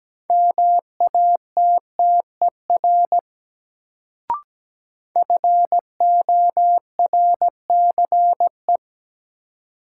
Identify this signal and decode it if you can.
Morse code, heard as a keyed tone, about 17 wpm, 700 Hz. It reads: MATTER FORCE
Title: Sentences from Top 700 Words - New Word Drill - 17wpm